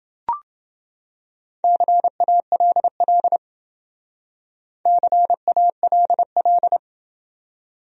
Morse code, heard as a keyed tone, about 28 wpm, 700 Hz. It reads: CALL CALL